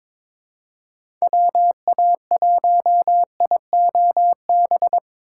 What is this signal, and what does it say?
Morse code, heard as a keyed tone, about 21 wpm, 700 Hz. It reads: WA1IOB
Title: USA Call Signs - 6 Characters - 2x Character Spacing - 22wpm